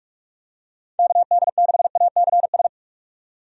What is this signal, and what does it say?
Morse code, heard as a keyed tone, about 46 wpm, 700 Hz. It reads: KD6ACS
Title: USA Call Signs - 6 Characters - 45wpm